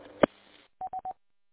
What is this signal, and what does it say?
There is no code.